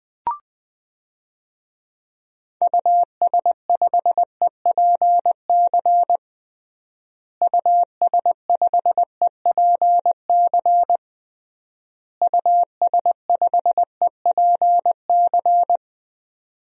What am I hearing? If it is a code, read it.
Morse code, heard as a keyed tone, about 20 wpm, 700 Hz. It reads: US5EPC US5EPC US5EPC
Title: DXCC-100 Call Signs - 6 Characters - Repeated 3x - 20wpm